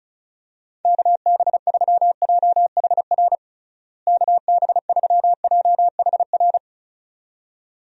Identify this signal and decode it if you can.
Morse code, heard as a keyed tone, about 36 wpm, 700 Hz. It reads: KB3JHR KB3JHR
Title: USA Call Signs - 6 Characters - Repeated 2x - 35wpm